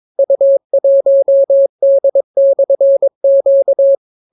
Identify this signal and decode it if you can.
Morse code, heard as a keyed tone, about 22 wpm, 550 Hz. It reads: U1D/Q